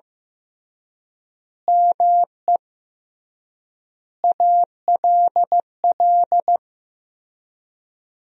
Morse code, heard as a keyed tone, about 15 wpm, 700 Hz. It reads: ME ALL
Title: 3x Word-Spacing - Sentences from Top 400 Words - 15wpm